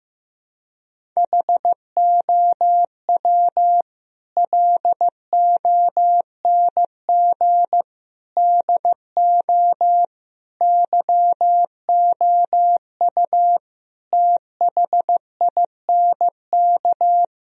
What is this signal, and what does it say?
Morse code, heard as a keyed tone, about 15 wpm, 700 Hz. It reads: HOW LONG DO YOU THINK
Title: Sentences from Top 1000 Words - Review Part 2 - 15wpm